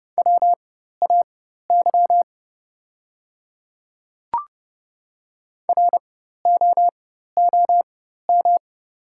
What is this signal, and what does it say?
Morse code, heard as a keyed tone, about 31 wpm, spaced as for 14 wpm, 700 Hz. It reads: WAY ROOM
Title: 4x Character Spacing - Top 500 Words - 30wpm